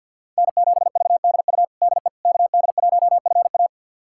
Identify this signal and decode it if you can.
Morse code, heard as a keyed tone, about 52 wpm, 700 Hz. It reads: N7VDU DE KD1FA
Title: USA Callsign DE Callsign - 5 Characters - 50wpm